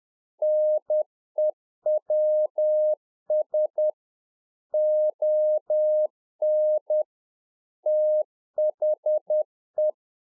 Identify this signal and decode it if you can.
Morse code, heard as a keyed tone, about 10 wpm, 610 Hz. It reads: NEWS ON THE